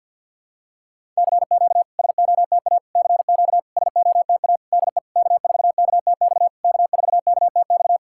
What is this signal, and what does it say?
Morse code, heard as a keyed tone, about 51 wpm, 690 Hz. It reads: CQ SOTA CQ SOTA DE K4KTX K4KTX
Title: SOTA QSO Practice 50wpm